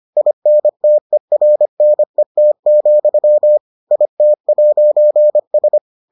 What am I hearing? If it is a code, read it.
Morse code, heard as a keyed tone, about 25 wpm, 600 Hz. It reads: INTERNET, IT'S